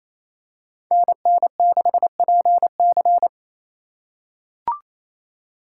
Morse code, heard as a keyed tone, about 28 wpm, 700 Hz. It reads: NN6PC
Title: USA Call Signs - 3x Character Spacing - 28wpm